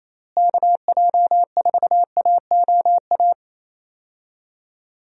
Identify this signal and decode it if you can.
Morse code, heard as a keyed tone, about 28 wpm, 700 Hz. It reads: KJ4AOA
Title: USA Call Signs - 6 Characters - Repeated 2x - 28wpm